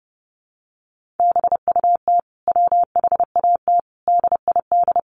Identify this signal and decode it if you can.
Morse code, heard as a keyed tone, about 30 wpm, 700 Hz. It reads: BUT WHAT DID
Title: Sentences from Top 100 Words 30wpm